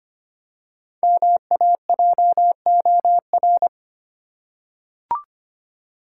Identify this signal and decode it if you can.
Morse code, heard as a keyed tone, about 25 wpm, 700 Hz. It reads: MAJOR
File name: 5 Letter Words - First 3 Letters Given - 25wpm